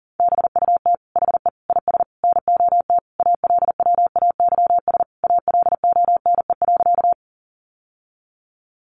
Morse code, heard as a keyed tone, about 37 wpm, 700 Hz. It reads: BUT HE IS NOT ALWAYS ALONE.
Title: Sentences from Top 1000 Words 40wpm